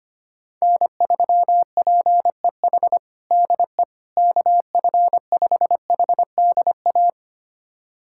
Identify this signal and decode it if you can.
Morse code, heard as a keyed tone, about 25 wpm, 700 Hz. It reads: N3PEH DE KF5HDA
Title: USA Callsign DE Callsign 25wpm